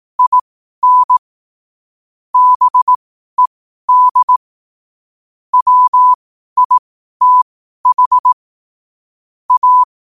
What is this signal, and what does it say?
Morse code, heard as a keyed tone, about 18 wpm, spaced as for 12 wpm, 1000 Hz. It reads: IN BED WITH A
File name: qotdfiles / 12wpm